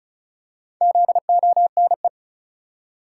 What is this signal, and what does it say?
Morse code, heard as a keyed tone, about 36 wpm, 700 Hz. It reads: ZONE
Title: Four Letter Words - 35wpm